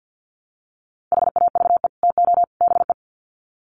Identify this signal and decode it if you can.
Morse code, heard as a keyed tone, about 50 wpm, 700 Hz. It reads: HAVE TO BE
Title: Sets of 3 Words 50wpm